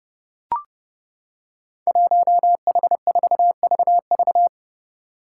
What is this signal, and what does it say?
Morse code, heard as a keyed tone, about 30 wpm, 700 Hz. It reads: 1H4VV
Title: Five Challenging Letter-Number Combinations 30wpm